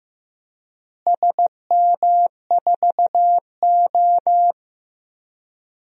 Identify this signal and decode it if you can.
Morse code, heard as a keyed tone, about 15 wpm, 700 Hz. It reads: SM4O
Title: DXCC-100 Call Signs - 4 Characters - Repeated 3x - 15wpm